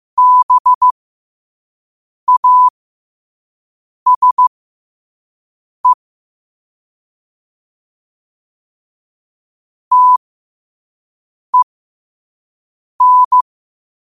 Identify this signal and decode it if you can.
Morse code, heard as a keyed tone, about 15 wpm, spaced as for 5 wpm, 1000 Hz. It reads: BASE TEN
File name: qotdfiles / 5wpm